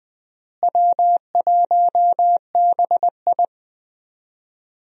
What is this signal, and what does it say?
Morse code, heard as a keyed tone, about 20 wpm, 700 Hz. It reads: W1BI
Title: USA Call Signs - 4 Characters - Repeated 3x - 20wpm